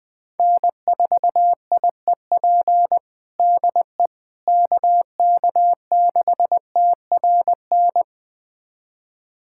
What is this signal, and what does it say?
Morse code, heard as a keyed tone, about 21 wpm, 700 Hz. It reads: N4IEP DE KK6TRN
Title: USA Callsign DE Callsign 20wpm